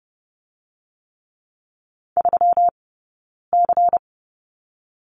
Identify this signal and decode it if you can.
Morse code, heard as a keyed tone, about 30 wpm, spaced as for 9 wpm, 700 Hz. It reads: 3C
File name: Lesson 20d - Number 3 - Two Characters 30wpm